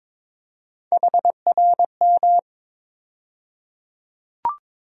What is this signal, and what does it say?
Morse code, heard as a keyed tone, about 21 wpm, 700 Hz. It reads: HRM
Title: USA Call Sign Postfixes - 3 Letters 22wpm